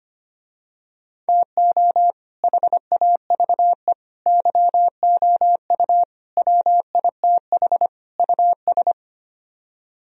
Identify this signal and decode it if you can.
Morse code, heard as a keyed tone, about 25 wpm, 700 Hz. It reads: TO HAVE YOU WITH US
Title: Speed-Racing - Sets of 5 Words 25wpm